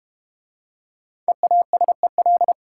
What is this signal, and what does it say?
Morse code, heard as a keyed tone, about 32 wpm, 700 Hz. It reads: EASEL